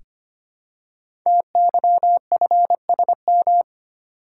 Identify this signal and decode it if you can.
Morse code, heard as a keyed tone, about 25 wpm, 700 Hz. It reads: TYFSM
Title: Code Groups - 5 Letters - 25wpm